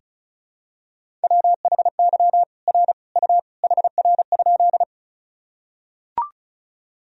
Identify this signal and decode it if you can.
Morse code, heard as a keyed tone, about 36 wpm, 700 Hz. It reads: WHY R U HR?
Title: Sentences from Top 200 Words and Abbreviations 35wpm